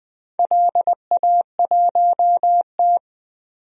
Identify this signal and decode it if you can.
Morse code, heard as a keyed tone, about 20 wpm, 700 Hz. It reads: LA1T
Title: DXCC-100 Call Signs - 4 Characters - 20wpm